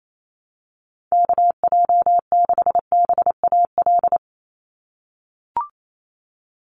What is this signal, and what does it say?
Morse code, heard as a keyed tone, about 28 wpm, 700 Hz. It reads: KJ6BAL
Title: USA Call Signs 28wpm